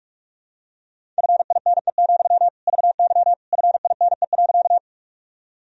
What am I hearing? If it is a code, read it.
Morse code, heard as a keyed tone, about 46 wpm, 700 Hz. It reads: FINE, VY FINE.
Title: Sentences from Top 500 Words and Abbreviations 45wpm